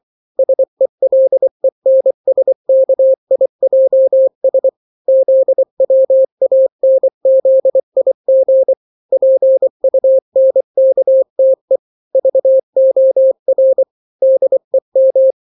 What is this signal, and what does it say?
Morse code, heard as a keyed tone, about 24 wpm, 530 Hz. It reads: SELENSKIJS ZWANZIG PUNKTE VOR DEM
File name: daily / 23wpm